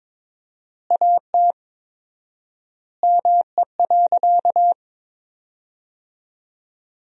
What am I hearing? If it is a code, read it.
Morse code, heard as a keyed tone, about 22 wpm, 700 Hz. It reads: AT ME.
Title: 4x Word-Spacing - Sentences from Top 200 Words - 22wpm